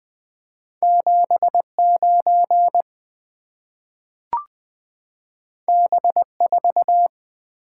Morse code, heard as a keyed tone, about 20 wpm, 700 Hz. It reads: 79 B4
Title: Two Letter-Number Combinations - 20wpm